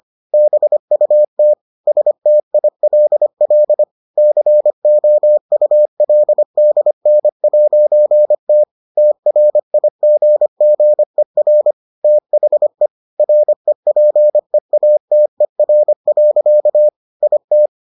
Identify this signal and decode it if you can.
Morse code, heard as a keyed tone, about 25 wpm, 600 Hz. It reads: BUT STILL COULDN'T TRIGGER THE REPEATER. IT